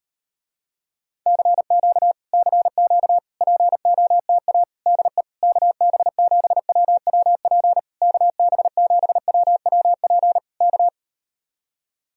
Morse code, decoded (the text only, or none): CQ CQ POTA DE KB7WWP KB7WWP K